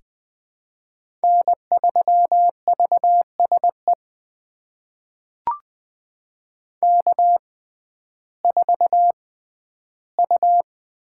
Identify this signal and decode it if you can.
Morse code, heard as a keyed tone, about 20 wpm, spaced as for 12 wpm, 700 Hz. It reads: N3VSE K4U
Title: USA Call Signs - 5 Characters - 6x Character Spacing - 20wpm